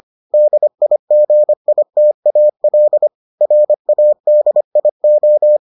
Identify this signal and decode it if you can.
Morse code, heard as a keyed tone, about 25 wpm, 600 Hz. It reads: DIGITAL RADIO